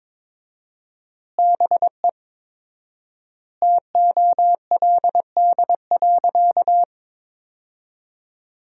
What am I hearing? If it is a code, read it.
Morse code, heard as a keyed tone, about 22 wpm, 700 Hz. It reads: BE TOLD.
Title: 4x Word-Spacing - Sentences from Top 400 Words - 22wpm